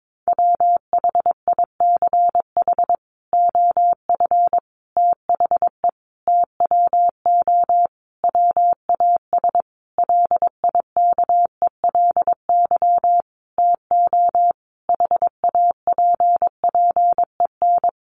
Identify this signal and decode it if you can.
Morse code, heard as a keyed tone, about 22 wpm, 700 Hz. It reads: WHICH OF THE TWO WAS LIKELY TO HAPPEN
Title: Sentences from Top 1300 Words 22wpm